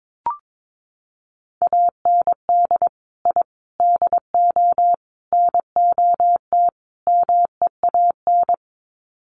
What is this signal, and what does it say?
Morse code, heard as a keyed tone, about 22 wpm, 700 Hz. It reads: AND I DO NOT MEAN
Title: Sets of 5 Words Encore2 22wpm